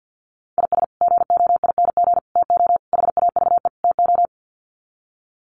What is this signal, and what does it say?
Morse code, heard as a keyed tone, about 51 wpm, 700 Hz. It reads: IS GOING TO HAVE TO
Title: Sets of 5 Words 50wpm